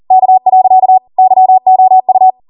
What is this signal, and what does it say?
Morse code, heard as a keyed tone, about 40 wpm, 750 Hz. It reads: K. YOU